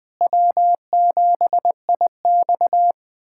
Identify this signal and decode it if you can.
Morse code, heard as a keyed tone, about 20 wpm, 700 Hz. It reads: W7IX
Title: USA Callsign DE Callsign - 4 Characters - 20wpm